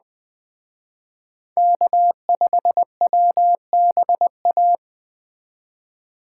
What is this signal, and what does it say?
Morse code, heard as a keyed tone, about 21 wpm, 700 Hz. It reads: K5WBA
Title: Speed Racing - USA 1x3 Call Signs 17wpm